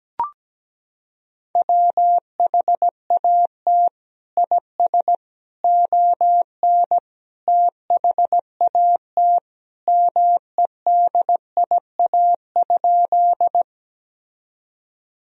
Morse code, heard as a keyed tone, about 17 wpm, 700 Hz. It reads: WHAT IS ON THAT MEDIA?
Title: Sentences from Top 1000 Words - Review Part 4 - 17wpm